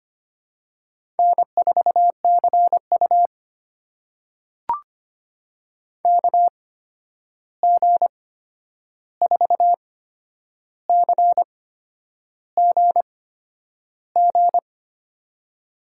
Morse code, heard as a keyed tone, about 25 wpm, spaced as for 9 wpm, 700 Hz. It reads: N4CU KG4CGG
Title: USA Call Signs - 8x Character Spacing - 25wpm